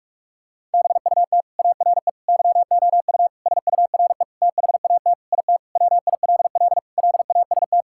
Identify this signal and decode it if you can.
Morse code, heard as a keyed tone, about 46 wpm, 700 Hz. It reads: BUT ARE YOU SURE THAT IT WILL LAST
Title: Sentences from Top 1600 Words 45wpm